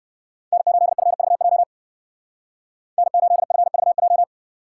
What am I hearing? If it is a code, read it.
Morse code, heard as a keyed tone, about 63 wpm, 700 Hz. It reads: N8FVJ N8FVJ